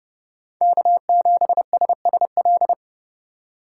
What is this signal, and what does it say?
Morse code, heard as a keyed tone, about 32 wpm, 700 Hz. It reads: K7SSL
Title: USA Call Signs - 5 Characters - 4x Character Spacing - 30wpm